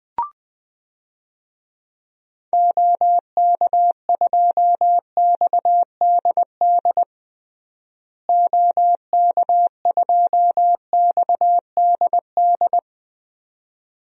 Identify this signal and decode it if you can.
Morse code, heard as a keyed tone, about 20 wpm, 700 Hz. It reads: OK2XDD OK2XDD
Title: DXCC-100 Call Signs - 6 Characters - Repeated 2x - 20wpm